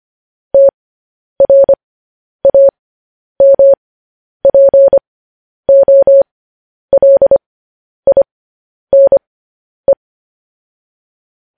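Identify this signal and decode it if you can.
Morse code, heard as a keyed tone, about 25 wpm, spaced as for 10 wpm, 550 Hz. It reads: TRAMPOLINE
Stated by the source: Morse code news headlines sent as a reading transcription, preceded by vvv.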